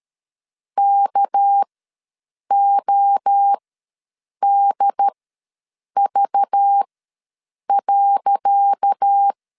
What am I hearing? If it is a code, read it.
Morse code, heard as a keyed tone, about 13 wpm, spaced as for 7 wpm, 800 Hz. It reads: KODV.
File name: LEZIONI / 10 SIGNS / 10C DOT ws 12wpm 3548